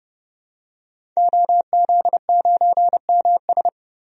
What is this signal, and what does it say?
Morse code, heard as a keyed tone, about 32 wpm, 700 Hz. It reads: OZ9MS
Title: DXCC-100 Call Signs - 5 Characters - 30wpm